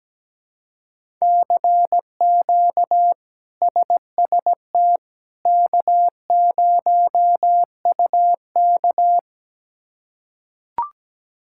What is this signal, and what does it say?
Morse code, heard as a keyed tone, about 17 wpm, 700 Hz. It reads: CQ SST K0UK
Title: K1USN SST QSO Practice - 17wpm